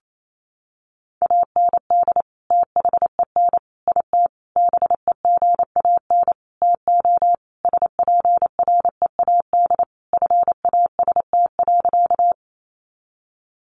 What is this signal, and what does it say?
Morse code, heard as a keyed tone, about 28 wpm, 700 Hz. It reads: AND THEN IT BEGAN TO SPREAD FAST.
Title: Sentences from Top 1200 Words 28wpm